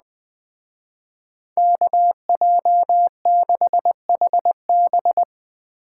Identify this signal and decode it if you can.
Morse code, heard as a keyed tone, about 20 wpm, 700 Hz. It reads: KJ6HB
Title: Speed Racing - USA Call Signs 17wpm